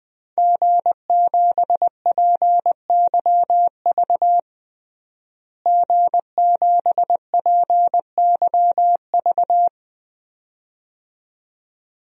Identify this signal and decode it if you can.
Morse code, heard as a keyed tone, about 21 wpm, 700 Hz. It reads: G7PYV G7PYV
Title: DXCC-100 Call Signs - 5 Characters - Repeated 2x - 20wpm